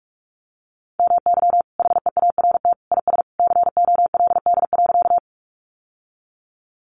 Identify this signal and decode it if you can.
Morse code, heard as a keyed tone, about 44 wpm, 700 Hz. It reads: MY HEART IS COLD.
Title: Sentences from Top 1200 Words 45wpm